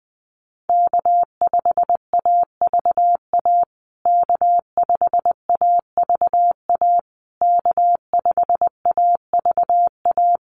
Morse code, heard as a keyed tone, about 20 wpm, 700 Hz. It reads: K5AVA K5AVA K5AVA
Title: USA Call Signs Repeated 3x 20wpm